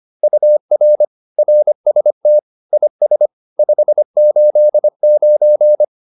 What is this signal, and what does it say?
Morse code, heard as a keyed tone, about 25 wpm, 600 Hz. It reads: UR RST IS 589